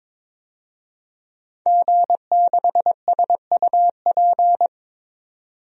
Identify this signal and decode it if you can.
Morse code, heard as a keyed tone, about 22 wpm, 700 Hz. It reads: G6SUP
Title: DXCC-100 Call Signs - 5 Characters - 22wpm